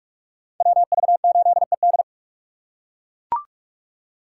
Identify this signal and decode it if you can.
Morse code, heard as a keyed tone, about 44 wpm, 700 Hz. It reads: WV8ED